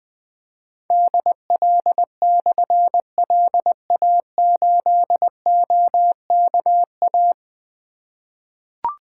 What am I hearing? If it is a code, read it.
Morse code, heard as a keyed tone, about 20 wpm, 700 Hz. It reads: DL/LA8OKA
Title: Difficult Call Signs 20wpm